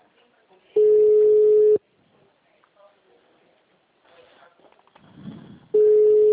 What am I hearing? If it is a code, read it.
Ringback tone (Europe, 425 Hz), with speech.